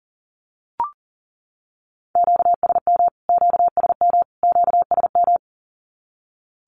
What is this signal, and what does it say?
Morse code, heard as a keyed tone, about 40 wpm, 700 Hz. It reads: QSM QSM QSM
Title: Q-Codes Repeated 3x 40wpm